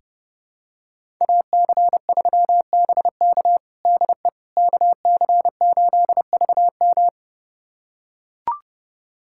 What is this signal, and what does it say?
Morse code, heard as a keyed tone, about 30 wpm, 700 Hz. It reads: AC3BK DE KC8VM
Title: USA Callsign DE Callsign - 5 Characters - 30wpm